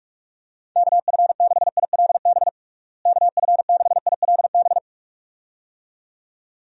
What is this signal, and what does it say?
Morse code, heard as a keyed tone, about 46 wpm, 700 Hz. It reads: KF6ILB KF6ILB
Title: USA Call Signs - 6 Characters - Repeated 2x - 45wpm